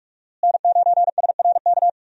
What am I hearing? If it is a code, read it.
Morse code, heard as a keyed tone, about 45 wpm, 700 Hz. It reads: N9SRK
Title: USA Call Signs - 5 Characters - Repeated 2x - 45wpm